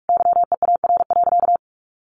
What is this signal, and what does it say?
Morse code, heard as a keyed tone, about 44 wpm, 700 Hz. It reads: YEAR.
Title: Sentences from Top 300 Words 45wpm